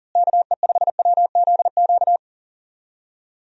Morse code, heard as a keyed tone, about 40 wpm, 700 Hz. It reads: KE5WZQ
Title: USA Call Signs - 6 Characters - Repeated 2x - 40wpm